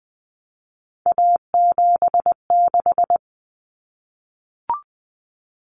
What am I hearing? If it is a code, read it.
Morse code, heard as a keyed tone, about 20 wpm, 700 Hz. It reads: A76
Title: Common Cut Numbers - 3-Digit 20wpm